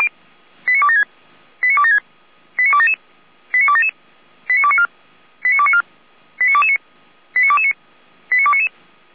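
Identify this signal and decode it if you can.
Five-tone selective call (ZVEI), heard as quick runs of five tones, tones of about 70 ms. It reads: 89276 89276 89277 89277 89294 89294 89229 89229 89299
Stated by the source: ZVEI five-tone selective call received by SDR.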